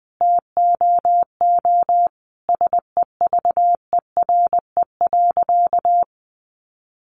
Morse code, heard as a keyed tone, about 20 wpm, 700 Hz. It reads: TOO SEVERE.
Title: Sentences from Top 1400 Words 20wpm